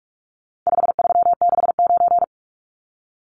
Morse code, heard as a keyed tone, about 44 wpm, 700 Hz. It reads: 5369
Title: Numbers - 4-Digit 45wpm